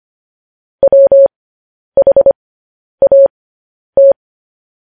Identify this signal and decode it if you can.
Morse code, heard as a keyed tone, about 25 wpm, spaced as for 10 wpm, 550 Hz. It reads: WHAT
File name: podcast / News headlines-1x-25x10